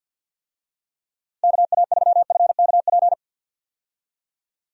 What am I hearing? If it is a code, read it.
Morse code, heard as a keyed tone, about 47 wpm, 700 Hz. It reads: KA3FKP
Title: USA Call Signs - 6 Characters - Repeated 3x - 50wpm